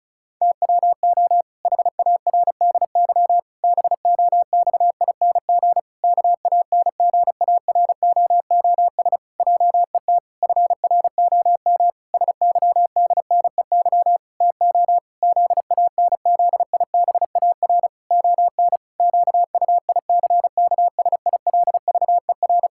Morse code, heard as a keyed tone, about 36 wpm, 700 Hz. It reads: TWO HARDY BOXING KANGAROOS JET FROM SYDNEY TO ZANZIBAR ON QUICKSILVER
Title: Pangram Challenge v2 - 35wpm